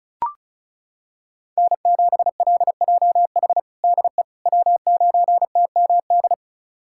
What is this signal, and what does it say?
Morse code, heard as a keyed tone, about 36 wpm, 700 Hz. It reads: N7LJH DE W9TMD
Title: USA Callsign DE Callsign - 5 Characters - 35wpm